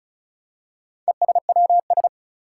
Morse code, heard as a keyed tone, about 36 wpm, 700 Hz. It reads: ESWS
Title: Four Challenging Letter-Number Combinations 35wpm